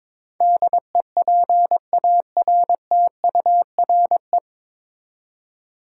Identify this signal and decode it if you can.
Morse code, heard as a keyed tone, about 22 wpm, 700 Hz. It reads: DEPARTURE
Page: Morse Code Ninja practice 22wpm